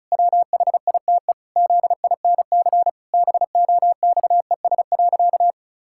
Morse code, heard as a keyed tone, about 36 wpm, 700 Hz. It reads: WHITE ZINC BOXES.
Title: Pangram Challenge v2 - 35wpm